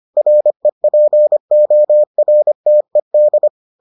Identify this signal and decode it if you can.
Morse code, heard as a keyed tone, about 25 wpm, 600 Hz. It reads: REPORTED